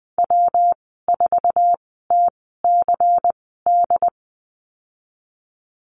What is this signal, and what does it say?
Morse code, heard as a keyed tone, about 19 wpm, spaced as for 14 wpm, 700 Hz. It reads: W4TCD
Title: 128 Lesson 21g - Number 6 - Call Signs 20wpm